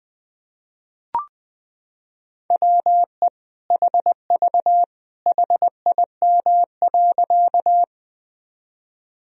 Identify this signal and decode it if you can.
Morse code, heard as a keyed tone, about 20 wpm, 700 Hz. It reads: WE HV HIM.
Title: Sentences from Top 100 Words and Abbreviations 20wpm